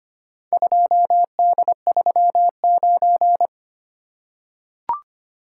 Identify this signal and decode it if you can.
Morse code, heard as a keyed tone, about 25 wpm, 700 Hz. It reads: 2D39
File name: practice / 25wpm / Four Letter-Number Combinations - 25wpm